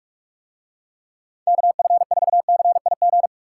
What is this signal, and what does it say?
Morse code, heard as a keyed tone, about 47 wpm, 700 Hz. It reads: KF4CIG